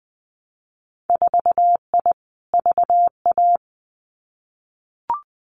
Morse code, heard as a keyed tone, about 20 wpm, 700 Hz. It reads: 4I VA
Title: Winter Field Day Exchanges 20wpm